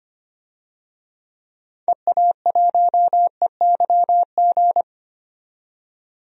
Morse code, heard as a keyed tone, about 25 wpm, 700 Hz. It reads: EA1EYG
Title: DXCC-100 Call Signs - 6 Characters - 25wpm